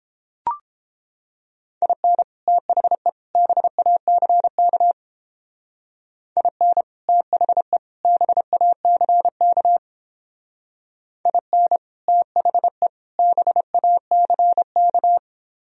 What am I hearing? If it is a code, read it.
Morse code, heard as a keyed tone, about 31 wpm, 700 Hz. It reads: IN THE BACK IN THE BACK IN THE BACK